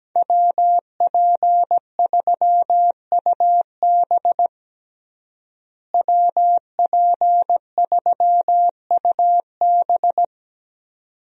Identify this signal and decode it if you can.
Morse code, heard as a keyed tone, about 17 wpm, 700 Hz. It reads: WP3UB WP3UB
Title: USA Call Signs - 5 Characters - Repeated 2x - 17wpm